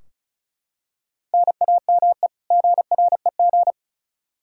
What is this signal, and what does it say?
Morse code, heard as a keyed tone, about 34 wpm, 700 Hz. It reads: NAME GREG
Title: Practice QSO - Exchange 1 - Beginning - 35wpm